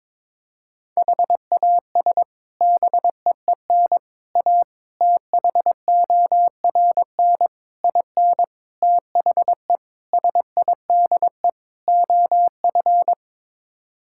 Morse code, heard as a keyed tone, about 22 wpm, 700 Hz. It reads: HAS BEEN A THORN IN THE SIDE OF
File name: practice / 22wpm / Sets of 8 Words - 22wpm